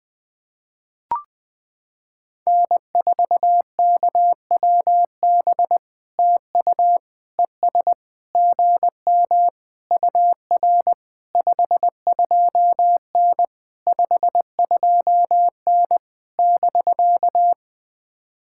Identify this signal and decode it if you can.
Morse code, heard as a keyed tone, about 20 wpm, 700 Hz. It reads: N4KWB TU ES GM UR 52N 52N <BK>
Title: POTA QSO Practice - 20wpm